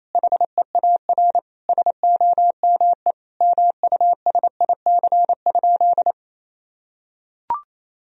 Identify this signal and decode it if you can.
Morse code, heard as a keyed tone, about 28 wpm, 700 Hz. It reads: HEAR SOME MUSIC?
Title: Sentences from Top 1000 Words - Review Part 1 - 28wpm